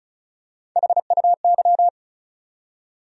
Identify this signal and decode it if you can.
Morse code, heard as a keyed tone, about 36 wpm, 700 Hz. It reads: HUY
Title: USA Call Sign Postfixes - 3 Letters 35wpm